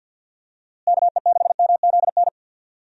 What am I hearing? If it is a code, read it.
Morse code, heard as a keyed tone, about 51 wpm, 700 Hz. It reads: KE6MZN
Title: USA Call Signs - 6 Characters - Repeated 3x - 50wpm